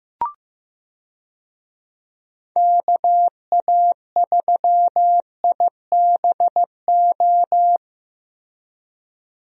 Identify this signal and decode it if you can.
Morse code, heard as a keyed tone, about 15 wpm, 700 Hz. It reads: KA3IBO
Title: USA Call Signs - 6 Characters - Repeated 2x - 15wpm